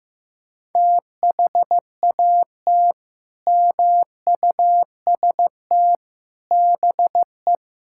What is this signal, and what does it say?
Morse code, heard as a keyed tone, about 15 wpm, 700 Hz. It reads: THAT MUST BE
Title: Sentences from Top 1000 Words - Review Part 5 - 15wpm